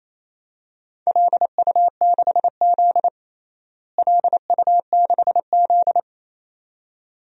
Morse code, heard as a keyed tone, about 28 wpm, 700 Hz. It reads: LU6Z LU6Z
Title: DXCC-100 Call Signs - 4 Characters - Repeated 2x - 28wpm